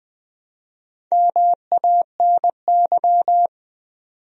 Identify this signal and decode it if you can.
Morse code, heard as a keyed tone, about 20 wpm, 700 Hz. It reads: MANY
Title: Top 100 Words 20wpm